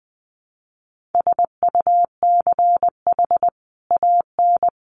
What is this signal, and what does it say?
Morse code, heard as a keyed tone, about 20 wpm, 700 Hz. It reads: SUCH AN
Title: Sentences from Top 1100 Words 20wpm